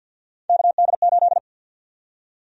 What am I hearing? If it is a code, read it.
Morse code, heard as a keyed tone, about 46 wpm, 700 Hz. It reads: KD8
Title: USA Call Sign Prefixes - Repeated 2x - 50wpm